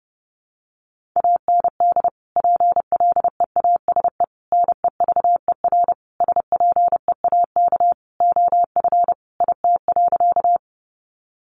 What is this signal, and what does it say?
Morse code, heard as a keyed tone, about 30 wpm, 700 Hz. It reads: AND PLEASE NEVER SPEAK OF IT.
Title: Sentences from Top 1500 Words 30wpm